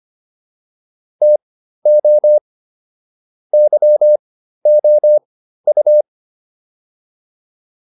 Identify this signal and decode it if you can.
Morse code, heard as a keyed tone, about 25 wpm, spaced as for 13 wpm, 600 Hz. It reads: TO YOU